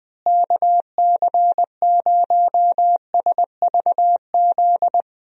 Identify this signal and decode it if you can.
Morse code, heard as a keyed tone, about 20 wpm, 700 Hz. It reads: KC0SVZ
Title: USA Call Signs - 6 Characters - Repeated 3x - 20wpm